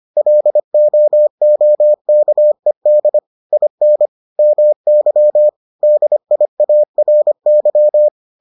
Morse code, heard as a keyed tone, about 25 wpm, 600 Hz. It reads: LOOKED IN MY DIARY